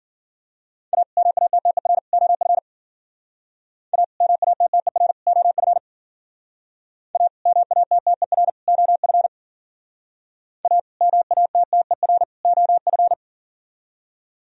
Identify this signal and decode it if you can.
Morse code, heard as a keyed tone, about 50 wpm, 700 Hz. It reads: A MATTER OF A MATTER OF A MATTER OF A MATTER OF
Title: Speed-Racing - Sets of 3 Words 40wpm